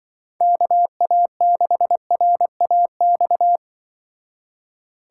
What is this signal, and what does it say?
Morse code, heard as a keyed tone, about 24 wpm, 700 Hz. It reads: KA6RAX
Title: Speed Racing - USA 2x3 Call Signs 20wpm